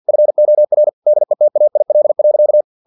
Morse code, heard as a keyed tone, about 50 wpm, 600 Hz. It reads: FOR DETAIL.
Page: Never 49wpm